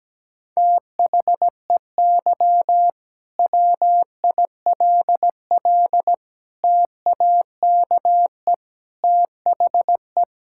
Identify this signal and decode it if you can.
Morse code, heard as a keyed tone, about 17 wpm, 700 Hz. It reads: THEY WILL TAKE THE
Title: Sentences from Top 1000 Words - Review Part 5 - 17wpm